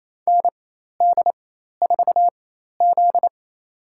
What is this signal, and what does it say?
Morse code, heard as a keyed tone, about 28 wpm, spaced as for 13 wpm, 700 Hz. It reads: ND4Z
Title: USA Call Signs - 4 Characters - 4x Character Spacing - 28wpm